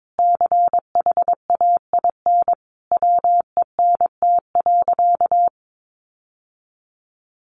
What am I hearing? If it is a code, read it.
Morse code, heard as a keyed tone, about 22 wpm, 700 Hz. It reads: CHAIN WENT.
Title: Sentences from Top 1500 Words 22wpm